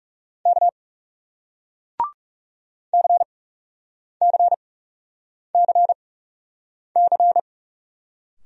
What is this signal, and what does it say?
Morse code, heard as a keyed tone, about 40 wpm, spaced as for 8 wpm, 700 Hz. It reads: K CCCC